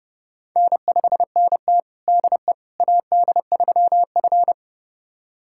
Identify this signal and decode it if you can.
Morse code, heard as a keyed tone, about 30 wpm, 700 Hz. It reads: N5NT DE AD3F
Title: USA Callsign DE Callsign - 4 Characters - 30wpm